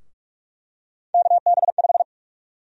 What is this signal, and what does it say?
Morse code, heard as a keyed tone, about 46 wpm, 700 Hz. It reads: KB5
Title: USA Call Sign Prefixes - 45wpm